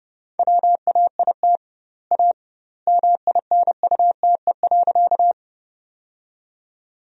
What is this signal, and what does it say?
Morse code, heard as a keyed tone, about 30 wpm, 700 Hz. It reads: WAIT A MINUTE.
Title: 2x Word-Spacing - Sentences from Top 500 Words - 30wpm